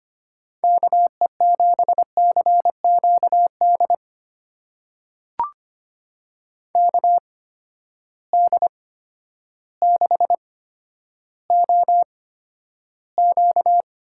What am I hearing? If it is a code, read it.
Morse code, heard as a keyed tone, about 25 wpm, spaced as for 11 wpm, 700 Hz. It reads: KE7CQD KD6OQ
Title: USA Call Signs - 6 Characters - 8x Character Spacing - 25wpm